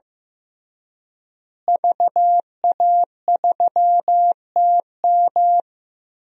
Morse code, heard as a keyed tone, about 15 wpm, 700 Hz. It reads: VA3TM
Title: DXCC-100 Call Signs - 5 Characters - Repeated 2x - 15wpm